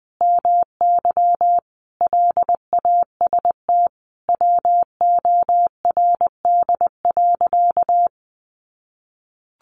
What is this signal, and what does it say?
Morse code, heard as a keyed tone, about 19 wpm, 700 Hz. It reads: MY LAST WORD.
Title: Sentences from Top 1200 Words 20wpm